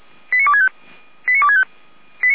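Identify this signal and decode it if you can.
Five-tone selective call (ZVEI), heard as quick runs of five tones, tones of about 70 ms. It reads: 89265 89265
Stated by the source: ZVEI five-tone selective call received on an SDR.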